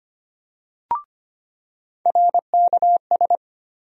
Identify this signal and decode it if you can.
Morse code, heard as a keyed tone, about 25 wpm, 700 Hz. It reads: RKS
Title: USA Call Sign Postfixes - 3 Letters 25wpm